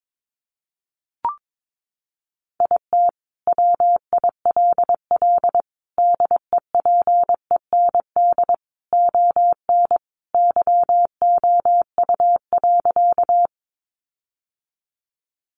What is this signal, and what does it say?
Morse code, heard as a keyed tone, about 22 wpm, 700 Hz. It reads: IT WILL DEPEND ON YOU.
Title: Sentences from Top 1300 Words 22wpm